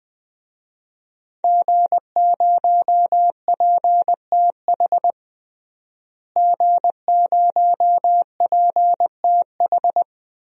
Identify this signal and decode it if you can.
Morse code, heard as a keyed tone, about 20 wpm, 700 Hz. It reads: G0PTH G0PTH
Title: DXCC-100 Call Signs - 5 Characters - Repeated 2x - 20wpm